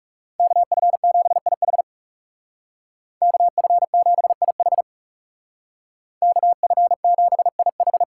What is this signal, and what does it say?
Morse code, heard as a keyed tone, about 39 wpm, 700 Hz. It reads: KF7IH KF7IH KF7IH